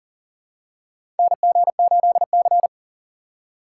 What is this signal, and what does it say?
Morse code, heard as a keyed tone, about 40 wpm, 700 Hz. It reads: NG8C
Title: USA Call Signs - 4 Characters - 3x Character Spacing - 40wpm